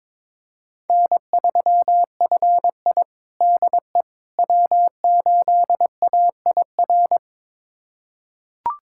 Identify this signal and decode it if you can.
Morse code, heard as a keyed tone, about 21 wpm, 700 Hz. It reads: N3FI DE W8AIR
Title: USA Callsign DE Callsign 22wpm